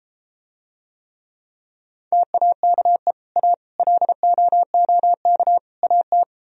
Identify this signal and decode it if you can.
Morse code, heard as a keyed tone, about 33 wpm, 700 Hz. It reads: TAKE A LOOK AT